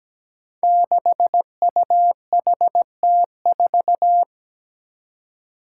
Morse code, heard as a keyed tone, about 17 wpm, 700 Hz. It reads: 6UHT4
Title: Five Challenging Letter-Number Combinations 17wpm